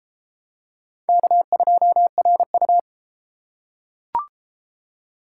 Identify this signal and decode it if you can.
Morse code, heard as a keyed tone, about 32 wpm, 700 Hz. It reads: K2RU